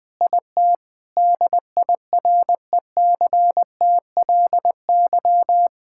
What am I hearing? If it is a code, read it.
Morse code, heard as a keyed tone, about 20 wpm, 700 Hz. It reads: IT DIRECTLY
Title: Sentences from Top 1000 Words - Review Part 5 - 20wpm